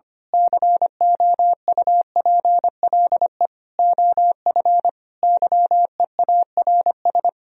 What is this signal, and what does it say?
Morse code, heard as a keyed tone, about 24 wpm, 700 Hz. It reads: COUPLE OF YEARS
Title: Speed-Racing - Sets of 5 Words 25wpm